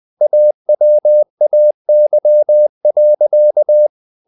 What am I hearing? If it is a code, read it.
Morse code, heard as a keyed tone, about 20 wpm, 600 Hz. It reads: AWAY.
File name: the old man and the sea / the old man and the sea part 1 20wpm